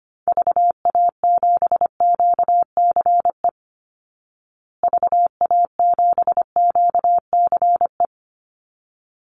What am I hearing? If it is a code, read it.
Morse code, heard as a keyed tone, about 25 wpm, 700 Hz. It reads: VA7QCE VA7QCE
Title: Call Signs Repeated 3x - 25wpm